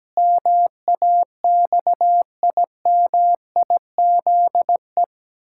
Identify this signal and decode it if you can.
Morse code, heard as a keyed tone, about 17 wpm, 700 Hz. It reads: MAXIMIZE
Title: Pangram Challenge v2 - 17wpm